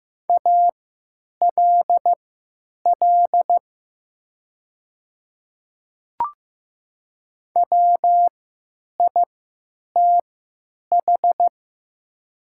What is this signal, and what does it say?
Morse code, heard as a keyed tone, about 15 wpm, spaced as for 9 wpm, 700 Hz. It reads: ALL WITH
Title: 3x Character Spacing - Top 100 Words - 15wpm